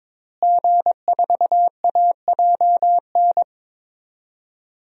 Morse code, heard as a keyed tone, about 22 wpm, 700 Hz. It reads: G4AJN
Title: DXCC-100 Call Signs - 5 Characters - Repeated 2x - 22wpm